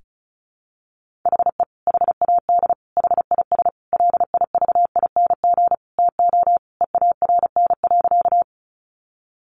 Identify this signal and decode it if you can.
Morse code, heard as a keyed tone, about 34 wpm, 700 Hz. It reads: HE HAD HIS LIVING TO EARN.
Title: Sentences from Top 1500 Words 35wpm